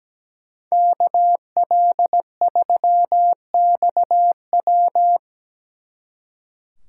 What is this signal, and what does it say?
Morse code, heard as a keyed tone, about 20 wpm, 700 Hz. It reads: KL3XW
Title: Speed Racing - USA Call Signs 17wpm